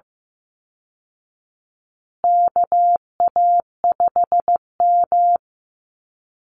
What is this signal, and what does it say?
Morse code, heard as a keyed tone, about 15 wpm, 700 Hz. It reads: KA5M
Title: Call Signs Repeated 2x - 15wpm